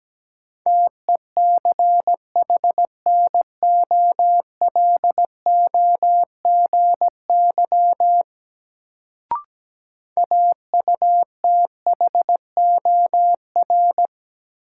Morse code, heard as a keyed tone, about 17 wpm, 700 Hz. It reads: TECHNOLOGY AUTHOR
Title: Sentences from Top 1200 Words - New Word Drill Alt - 17wpm